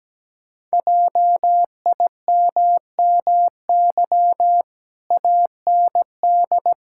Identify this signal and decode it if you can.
Morse code, heard as a keyed tone, about 17 wpm, 700 Hz. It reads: JIMMY AND
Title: Pangram Challenge v2 - 17wpm